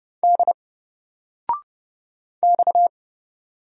Morse code, heard as a keyed tone, about 30 wpm, 700 Hz. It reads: D X
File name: Single-Letter-Number-30wpm